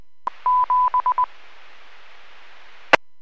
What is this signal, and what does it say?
Morse code, heard as a keyed tone, about 20 wpm, 1000 Hz. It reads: 7 E
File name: WPS / 00266 15-05 WPS 451.25 WPS